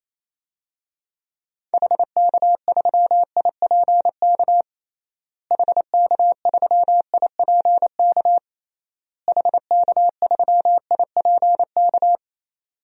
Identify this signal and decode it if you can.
Morse code, heard as a keyed tone, about 28 wpm, 700 Hz. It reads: HK3IPK HK3IPK HK3IPK
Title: DXCC-100 Call Signs - 6 Characters - Repeated 3x - 28wpm